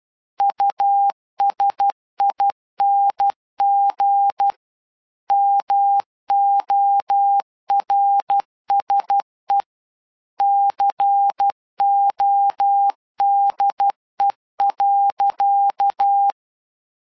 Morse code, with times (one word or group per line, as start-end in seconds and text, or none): 0.4-4.5 USING
5.3-9.6 MORSE
10.4-16.3 CODE.